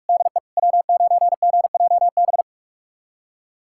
Morse code, heard as a keyed tone, about 46 wpm, 700 Hz. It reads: DE W9GJB
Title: USA Callsign DE Callsign - 5 Characters - 45wpm